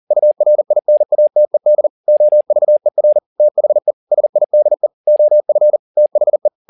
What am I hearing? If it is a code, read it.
Morse code, heard as a keyed tone, about 40 wpm, 600 Hz. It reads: URINATED OVER THE SIDE OF THE